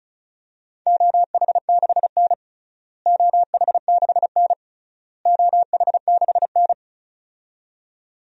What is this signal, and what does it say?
Morse code, heard as a keyed tone, about 34 wpm, 700 Hz. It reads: OH6N OH6N OH6N
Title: DXCC-100 Call Signs - 4 Characters - Repeated 3x - 35wpm